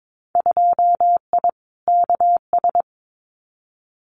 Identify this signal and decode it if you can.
Morse code, heard as a keyed tone, about 22 wpm, 700 Hz. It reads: 2I KS
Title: Winter Field Day Exchanges Encore 22wpm